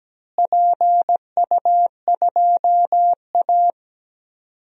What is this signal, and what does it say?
Morse code, heard as a keyed tone, about 17 wpm, 700 Hz. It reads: PU2A
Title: DXCC-100 Call Signs - 4 Characters - Repeated 2x - 17wpm